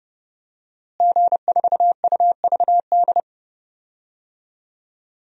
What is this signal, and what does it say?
Morse code, heard as a keyed tone, about 32 wpm, 700 Hz. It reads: G4UVD
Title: DXCC-100 Call Signs - 5 Characters - Repeated 2x - 30wpm